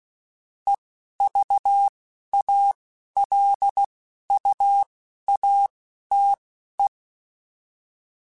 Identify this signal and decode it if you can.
Morse code, heard as a keyed tone, about 16 wpm, spaced as for 11 wpm, 800 Hz. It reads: EVALUATE